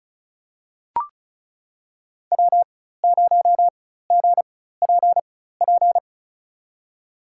Morse code, heard as a keyed tone, about 36 wpm, spaced as for 16 wpm, 700 Hz. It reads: W0GPP
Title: USA Call Signs - 5 Characters - 4x Character Spacing - 35wpm